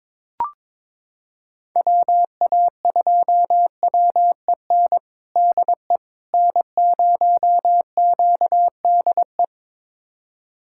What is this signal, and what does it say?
Morse code, heard as a keyed tone, about 22 wpm, 700 Hz. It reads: WA2WEN DE N0QDE
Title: USA Callsign DE Callsign 22wpm